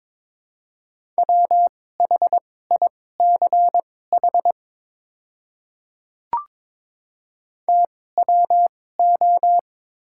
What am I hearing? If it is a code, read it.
Morse code, heard as a keyed tone, about 22 wpm, spaced as for 16 wpm, 700 Hz. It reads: WHICH TWO
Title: 2x Character Spacing - Top 100 Words - 22wpm